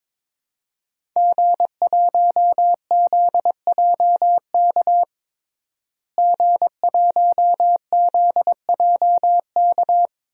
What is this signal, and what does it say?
Morse code, heard as a keyed tone, about 22 wpm, 700 Hz. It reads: G1ZJK G1ZJK
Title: DXCC-100 Call Signs - 5 Characters - Repeated 3x - 22wpm